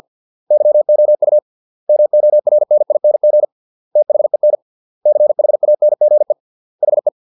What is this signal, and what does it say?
Morse code, heard as a keyed tone, about 50 wpm, 600 Hz. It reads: YOU MORNING THEN CHANGE HE